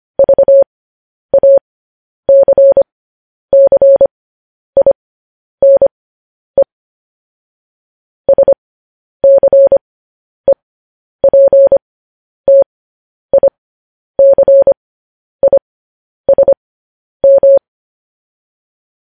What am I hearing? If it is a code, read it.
Morse code, heard as a keyed tone, about 25 wpm, spaced as for 10 wpm, 550 Hz. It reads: VACCINE SCEPTICISM
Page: news headlines in morse code at 10 wpm